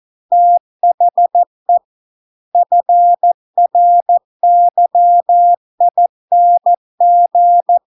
Morse code, heard as a keyed tone, about 14 wpm, 700 Hz. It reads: THE FRYING